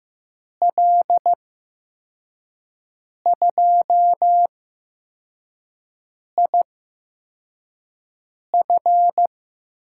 Morse code, heard as a keyed tone, about 15 wpm, spaced as for 4 wpm, 700 Hz. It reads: L2IF
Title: USA Call Signs - 5 Characters - 8x Character Spacing - 15wpm